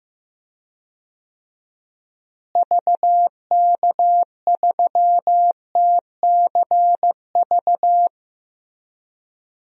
Morse code, heard as a keyed tone, about 15 wpm, 700 Hz. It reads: VK3TCV
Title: DXCC-100 Call Signs - 6 Characters - Repeated 2x - 15wpm